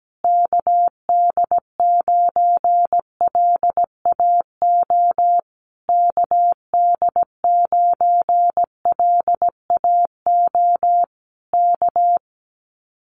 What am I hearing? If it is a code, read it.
Morse code, heard as a keyed tone, about 17 wpm, 700 Hz. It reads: KD9LAO KD9LAO K
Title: Common QSO Phrases Encore 17wpm